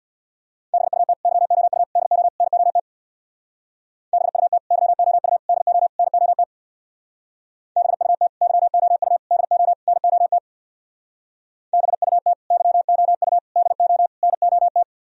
Morse code, heard as a keyed tone, about 60 wpm, 700 Hz. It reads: BUT YOU DO NOT BUT YOU DO NOT BUT YOU DO NOT BUT YOU DO NOT
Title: Speed-Racing - Sets of 4 Words 50wpm